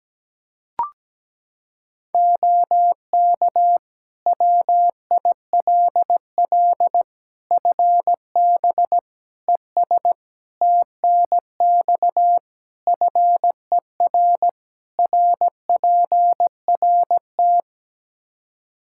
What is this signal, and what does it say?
Morse code, heard as a keyed tone, about 17 wpm, 700 Hz. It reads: OK WILL FB ES TNX FER RPRT
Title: Practice QSO - Exchange 2 - Info - 17wpm